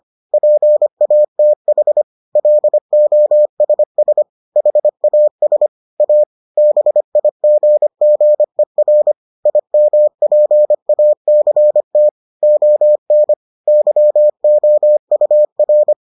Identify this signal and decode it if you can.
Morse code, heard as a keyed tone, about 25 wpm, 600 Hz. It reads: PATH LOSS HAS A BIGGER IMPACT ON YOUR